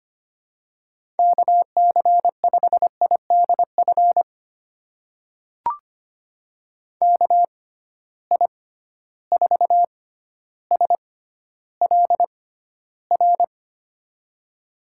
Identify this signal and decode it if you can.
Morse code, heard as a keyed tone, about 25 wpm, spaced as for 13 wpm, 700 Hz. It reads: KC5IDF KI4SLR
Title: USA Call Signs - 6 Characters - 6x Character Spacing - 25wpm